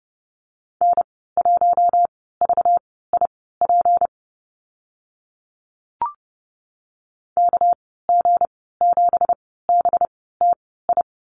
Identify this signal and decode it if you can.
Morse code, heard as a keyed tone, about 30 wpm, spaced as for 17 wpm, 700 Hz. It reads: N1VIP KG7BTI
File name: Lesson 37g - Letter x - Call Signs 30wpm